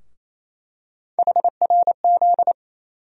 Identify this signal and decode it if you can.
Morse code, heard as a keyed tone, about 28 wpm, 700 Hz. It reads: HRZ
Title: USA Call Sign Postfixes - 3 Letters 28wpm